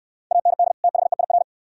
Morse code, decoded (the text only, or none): AND THEIR